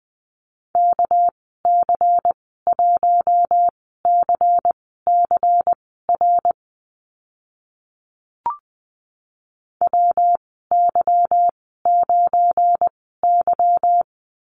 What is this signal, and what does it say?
Morse code, heard as a keyed tone, about 20 wpm, spaced as for 14 wpm, 700 Hz. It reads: KC1CCR WY9Y
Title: 219 Lesson 35g - Number 0 - Call Signs 20wpm